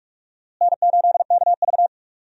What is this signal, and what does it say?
Morse code, heard as a keyed tone, about 46 wpm, 700 Hz. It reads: N8KV
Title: USA Call Signs - 4 Characters - Repeated 3x - 45wpm